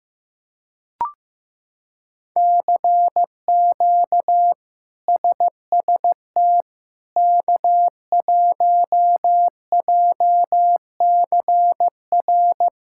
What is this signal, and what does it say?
Morse code, heard as a keyed tone, about 15 wpm, 700 Hz. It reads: CQ SST K1JCR
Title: K1USN SST QSO Practice - 15wpm